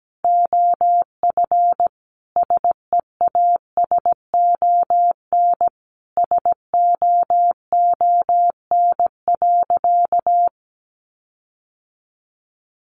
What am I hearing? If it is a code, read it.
Morse code, heard as a keyed tone, about 17 wpm, 700 Hz. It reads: OF SEASON SOON.
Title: Sentences from Top 1400 Words 17wpm